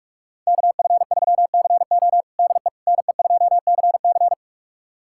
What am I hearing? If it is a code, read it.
Morse code, heard as a keyed tone, about 46 wpm, 700 Hz. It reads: KF3CO DE NE2CC
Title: USA Callsign DE Callsign 45wpm